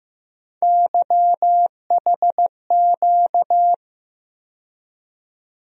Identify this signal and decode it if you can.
Morse code, heard as a keyed tone, about 16 wpm, 700 Hz. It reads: YHQ
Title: USA Call Sign Postfixes - 3 Letters 15wpm